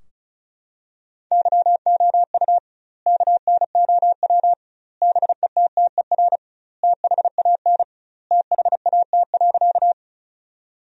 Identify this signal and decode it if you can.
Morse code, heard as a keyed tone, about 34 wpm, 700 Hz. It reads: YOU KNOW BETTER THAN THAT.
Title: 2x Word-Spacing - Sentences from Top 400 Words - 35wpm